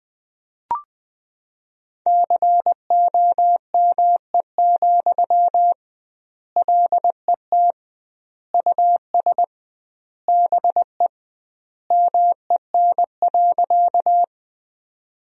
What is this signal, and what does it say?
Morse code, heard as a keyed tone, about 20 wpm, 700 Hz. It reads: COME, LET US BE MEN.
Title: 2x Word-Spacing - Sentences from Top 300 Words - 20wpm